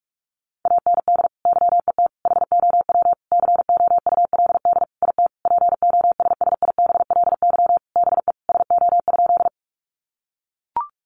Morse code, heard as a keyed tone, about 44 wpm, 700 Hz. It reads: AND YET HOW COULD IT POSSIBLY BE SO?
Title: Sentences from Top 1300 Words 45wpm